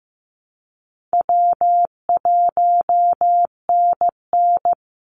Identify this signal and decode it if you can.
Morse code, heard as a keyed tone, about 15 wpm, 700 Hz. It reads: W1NN
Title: Easy Call Signs Repeated 3x - 15wpm